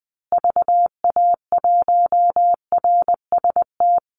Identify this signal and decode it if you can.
Morse code, heard as a keyed tone, about 20 wpm, 700 Hz. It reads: VA1RST
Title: Call Signs Repeated 2x - 20wpm